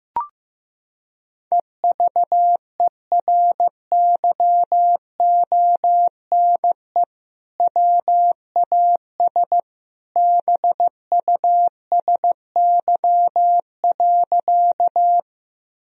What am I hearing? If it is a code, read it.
Morse code, heard as a keyed tone, about 16 wpm, 700 Hz. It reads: EVERYONE WAS BUSY.
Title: Sentences from Top 1000 Words - Review Part 3 - 15wpm